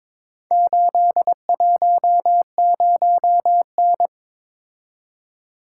Morse code, heard as a keyed tone, about 22 wpm, 700 Hz. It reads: 810N
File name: practice / 22wpm / Four Letter-Number Combinations - 22wpm